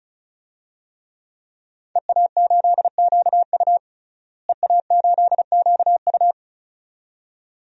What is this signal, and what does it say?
Morse code, heard as a keyed tone, about 36 wpm, 700 Hz. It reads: EA8QU EA8QU
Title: DXCC-100 Call Signs - 5 Characters - Repeated 2x - 35wpm